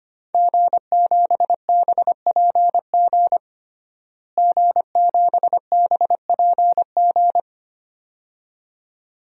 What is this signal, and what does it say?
Morse code, heard as a keyed tone, about 25 wpm, 700 Hz. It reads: G7BPG G7BPG
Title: DXCC-100 Call Signs - 5 Characters - Repeated 2x - 25wpm